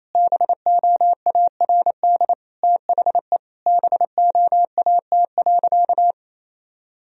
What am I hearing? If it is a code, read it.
Morse code, heard as a keyed tone, about 28 wpm, 700 Hz. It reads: BOARD THE BOAT.
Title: Sentences from Top 1000 Words - Review Part 4 - 28wpm